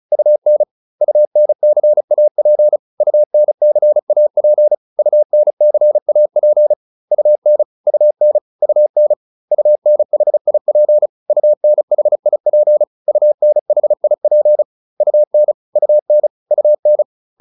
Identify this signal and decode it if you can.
Morse code, heard as a keyed tone, about 34 wpm, 600 Hz. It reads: UN UNCAP UNCAP UNCAP UN UN UN UNHIP UNHIP UNHIP UN UN UN
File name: UN5 35